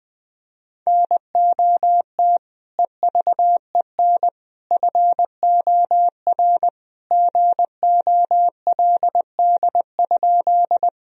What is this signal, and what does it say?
Morse code, heard as a keyed tone, about 21 wpm, 700 Hz. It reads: NOT EVEN FOR GOLD?
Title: Sentences from Top 1000 Words - Review Part 3 - 20wpm